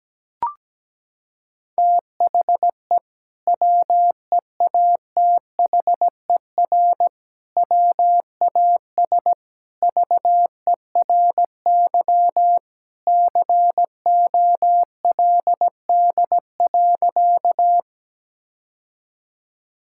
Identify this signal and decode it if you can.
Morse code, heard as a keyed tone, about 17 wpm, 700 Hz. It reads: THE WEATHER WAS VERY COLD.
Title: Sentences from Top 1000 Words - Review Part 5 - 17wpm